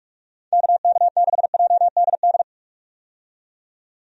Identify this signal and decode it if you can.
Morse code, heard as a keyed tone, about 46 wpm, 700 Hz. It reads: KK6JDD
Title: USA Call Signs - 6 Characters - 45wpm